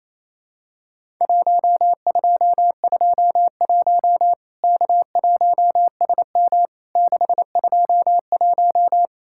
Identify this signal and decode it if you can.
Morse code, heard as a keyed tone, about 28 wpm, 700 Hz. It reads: 1221 K1SM 621
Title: Worked All Europe DX Contest - QTC Exchange - 28wpm